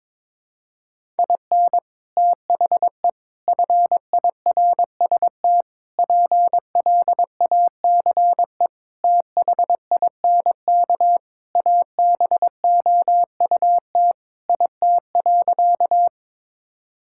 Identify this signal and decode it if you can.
Morse code, heard as a keyed tone, about 22 wpm, 700 Hz. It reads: IN THE FIRST PLACE THINK ABOUT IT.
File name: practice / 22wpm / Sentences with 3-Word Phrases - 22wpm